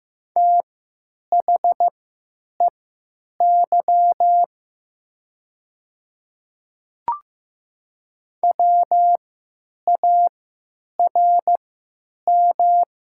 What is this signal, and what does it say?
Morse code, heard as a keyed tone, about 15 wpm, spaced as for 9 wpm, 700 Hz. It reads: THEY WARM
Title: 3x Character Spacing - Top 500 Words - 15wpm